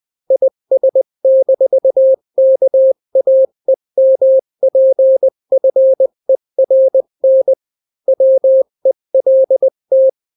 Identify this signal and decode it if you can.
Morse code, heard as a keyed tone, about 20 wpm, 530 Hz. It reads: IS-KAEMPFERN WELT
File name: daily / 18wpm